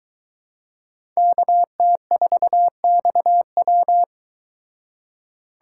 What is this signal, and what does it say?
Morse code, heard as a keyed tone, about 22 wpm, 700 Hz. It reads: KT4XW